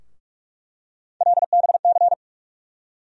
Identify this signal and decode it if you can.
Morse code, heard as a keyed tone, about 43 wpm, 700 Hz. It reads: LBC
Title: USA Call Sign Postfixes - 3 Letters 45wpm